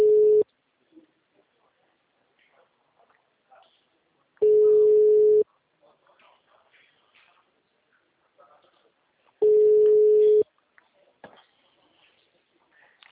Ringback tone (Europe, 425 Hz).